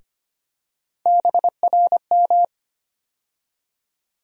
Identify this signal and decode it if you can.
Morse code, heard as a keyed tone, about 25 wpm, 700 Hz. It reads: BRM
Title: USA Call Sign Postfixes - 3 Letters 25wpm